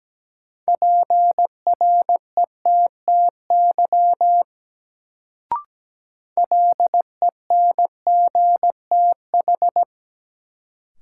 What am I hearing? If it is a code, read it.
Morse code, heard as a keyed tone, about 17 wpm, 700 Hz. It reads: PRETTY LENGTH
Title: Sentences from Top 1000 Words - New Word Drill - 17wpm